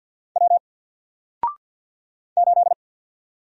Morse code, heard as a keyed tone, about 48 wpm, 700 Hz. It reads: W 8
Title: Single Letter-Number Alt 50wpm